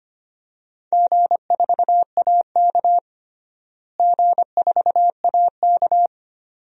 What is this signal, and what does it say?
Morse code, heard as a keyed tone, about 25 wpm, 700 Hz. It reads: G4AK G4AK
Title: DXCC-100 Call Signs - 4 Characters - Repeated 2x - 25wpm